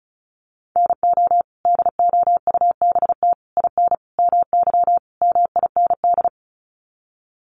Morse code, heard as a keyed tone, about 34 wpm, 700 Hz. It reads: NO DOUBT IN MY MIND
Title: Sets of 5 Words Encore2 35wpm